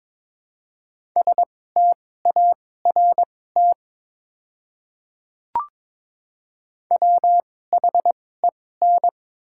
Morse code, heard as a keyed tone, about 22 wpm, spaced as for 16 wpm, 700 Hz. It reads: START WHEN
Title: 2x Character Spacing - Top 500 Words - 22wpm